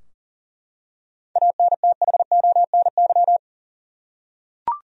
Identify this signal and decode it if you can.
Morse code, heard as a keyed tone, about 40 wpm, 700 Hz. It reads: ANTHONY
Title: USA Common Names - 40wpm